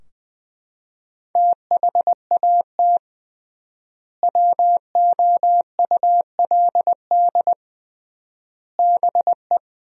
Morse code, heard as a keyed tone, about 21 wpm, 700 Hz. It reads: THAT WOULD BE
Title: 3x Word-Spacing - Sentences from Top 400 Words - 20wpm